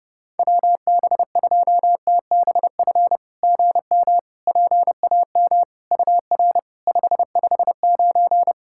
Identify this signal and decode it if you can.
Morse code, heard as a keyed tone, about 30 wpm, 690 Hz. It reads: WB2TBF GM PAM UR 559
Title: SOTA QSO Practice 30wpm